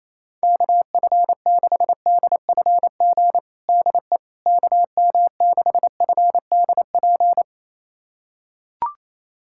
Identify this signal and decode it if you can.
Morse code, heard as a keyed tone, about 28 wpm, 700 Hz. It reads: KF6DFG DE KM6FDP
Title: USA Callsign DE Callsign - 6 Characters - 28wpm